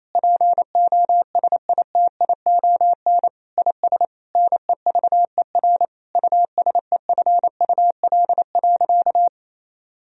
Morse code, heard as a keyed tone, about 28 wpm, 700 Hz. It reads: POSITION IS NEVER USEFUL.
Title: Sentences from Top 1000 Words - Review Part 4 - 28wpm